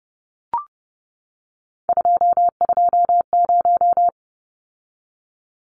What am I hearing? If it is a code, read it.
Morse code, heard as a keyed tone, about 30 wpm, 700 Hz. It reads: 220
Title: Numbers - 3-Digit 30wpm